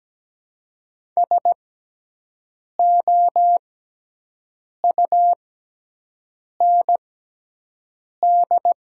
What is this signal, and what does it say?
Morse code, heard as a keyed tone, about 17 wpm, spaced as for 6 wpm, 700 Hz. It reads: SOUND
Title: 6x Character Spacing - Top 200 Words - 17wpm